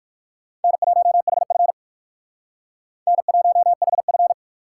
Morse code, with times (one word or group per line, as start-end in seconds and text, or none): 0.6-1.7 N1HF
3.1-4.3 N1HF